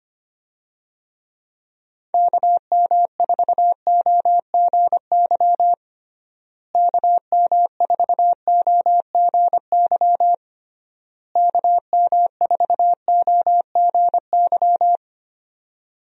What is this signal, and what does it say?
Morse code, heard as a keyed tone, about 25 wpm, 700 Hz. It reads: KM4OGY KM4OGY KM4OGY
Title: USA Call Signs - 6 Characters - Repeated 3x - 25wpm